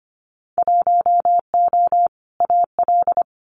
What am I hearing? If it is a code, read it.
Morse code, heard as a keyed tone, about 25 wpm, 700 Hz. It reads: 1O AL
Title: Winter Field Day Exchanges 25wpm